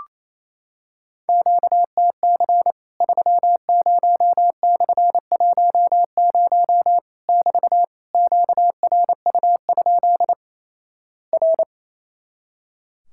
Morse code, heard as a keyed tone, about 28 wpm, 700 Hz. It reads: QTC 30/10 <BT> QRU? R
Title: Worked All Europe DX Contest - QTC Exchange - 28wpm